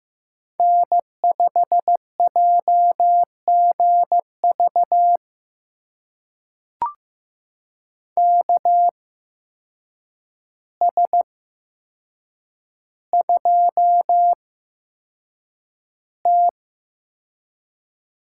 Morse code, heard as a keyed tone, about 15 wpm, spaced as for 7 wpm, 700 Hz. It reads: N5JGV KS2T
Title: USA Call Signs - 8x Character Spacing - 15wpm